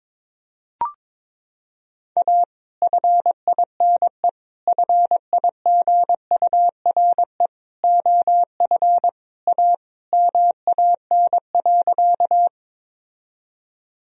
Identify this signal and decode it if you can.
Morse code, heard as a keyed tone, about 22 wpm, 700 Hz. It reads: A FINE FIGURE OF A MAN.
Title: Sentences from Top 1000 Words - Review Part 2 - 22wpm